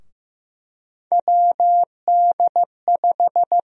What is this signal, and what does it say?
Morse code, heard as a keyed tone, about 15 wpm, 700 Hz. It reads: WD5
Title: USA Call Sign Prefixes - 15wpm